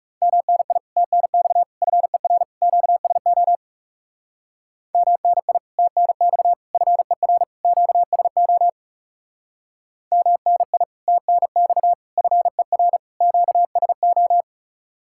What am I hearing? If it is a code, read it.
Morse code, heard as a keyed tone, about 37 wpm, 700 Hz. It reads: MNI TNX FER QSO MNI TNX FER QSO MNI TNX FER QSO
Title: Speed-Racing Common QSO Phrases Encore - 30wpm